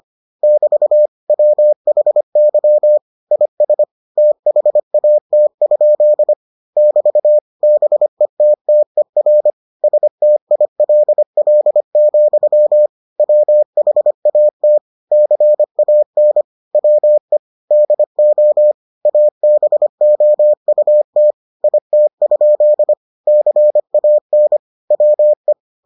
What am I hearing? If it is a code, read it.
Morse code, heard as a keyed tone, about 25 wpm, 600 Hz. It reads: <BT> WHY IS THAT? <BT> BETTER STILL, WHAT CAN WE DO ABOUT IT? CAN WE